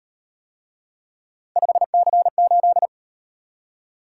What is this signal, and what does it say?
Morse code, heard as a keyed tone, about 38 wpm, 700 Hz. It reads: 5C8